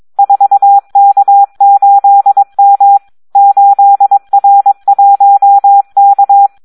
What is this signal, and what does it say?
Morse code, heard as a keyed tone, about 22 wpm, 800 Hz. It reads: 4K8M 8R1K
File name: Audio practica CW / SD 1 22